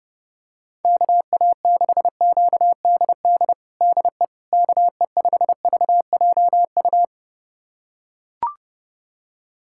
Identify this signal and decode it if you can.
Morse code, heard as a keyed tone, about 30 wpm, 700 Hz. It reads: KA6QDD DE KE5VJU
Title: USA Callsign DE Callsign 30wpm